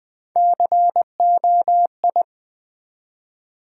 Morse code, heard as a keyed tone, about 20 wpm, 700 Hz. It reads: COI
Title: Build A Word - 1 Letter At A Time - 4-Letter Words - 20wpm